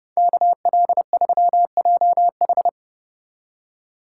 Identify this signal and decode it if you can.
Morse code, heard as a keyed tone, about 33 wpm, 700 Hz. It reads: KL3JH